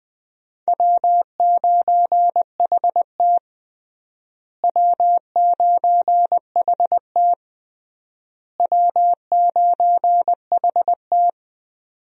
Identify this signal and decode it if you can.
Morse code, heard as a keyed tone, about 21 wpm, 700 Hz. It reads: W9HT W9HT W9HT
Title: USA Call Signs - 4 Characters - Repeated 3x - 20wpm